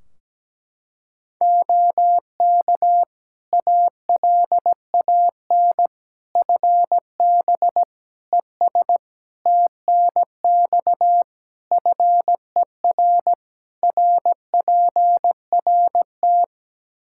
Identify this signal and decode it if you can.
Morse code, heard as a keyed tone, about 17 wpm, 700 Hz. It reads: OK ALAN FB ES TNX FER RPRT
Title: Practice QSO - Exchange 2 - Info - 17wpm